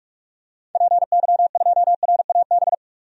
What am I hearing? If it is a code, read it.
Morse code, heard as a keyed tone, about 41 wpm, 690 Hz. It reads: PY2RAB
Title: DXCC-100 Call Signs - 6 Characters - 45wpm